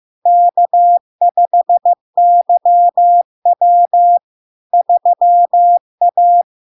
Morse code, heard as a keyed tone, about 15 wpm, 700 Hz. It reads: K5YW 3A
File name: LARK-FD-2022-Log-in-CW-15wpm-700-Hz